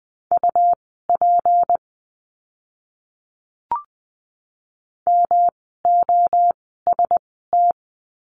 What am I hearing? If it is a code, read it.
Morse code, heard as a keyed tone, about 20 wpm, spaced as for 14 wpm, 700 Hz. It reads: UP MOST
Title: 244 Lesson 39f - Letter z - Words 20wpm